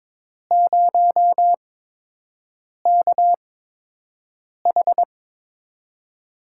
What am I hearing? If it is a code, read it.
Morse code, heard as a keyed tone, about 22 wpm, spaced as for 6 wpm, 700 Hz. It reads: 0KH
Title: USA Call Signs - 4 Characters - 8x Character Spacing - 22wpm